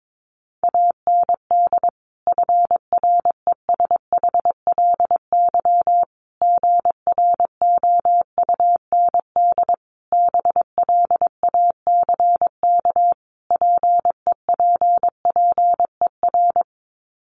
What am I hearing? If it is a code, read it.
Morse code, heard as a keyed tone, about 22 wpm, 700 Hz. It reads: AND FRESHLY GROUND BLACK PEPPER
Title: Sets of 5 Words Encore 22wpm